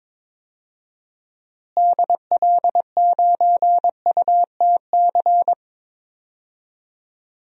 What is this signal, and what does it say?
Morse code, heard as a keyed tone, about 22 wpm, 700 Hz. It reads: DL9UTC
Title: DXCC-100 Call Signs - 6 Characters - Repeated 3x - 22wpm